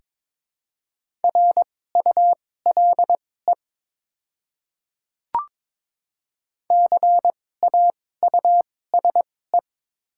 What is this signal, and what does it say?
Morse code, heard as a keyed tone, about 22 wpm, spaced as for 16 wpm, 700 Hz. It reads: RULE CAUSE
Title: 2x Character Spacing - Top 500 Words - 22wpm